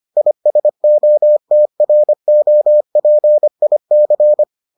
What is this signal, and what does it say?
Morse code, heard as a keyed tone, about 25 wpm, 600 Hz. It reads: ISOTROPIC